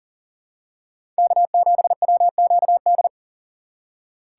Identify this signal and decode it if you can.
Morse code, heard as a keyed tone, about 40 wpm, 700 Hz. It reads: K7WQD
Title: USA Call Signs - 5 Characters - 3x Character Spacing - 40wpm